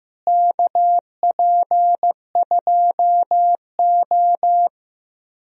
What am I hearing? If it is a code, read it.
Morse code, heard as a keyed tone, about 15 wpm, 700 Hz. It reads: KP2O
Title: DXCC-100 Call Signs - 4 Characters - Repeated 3x - 15wpm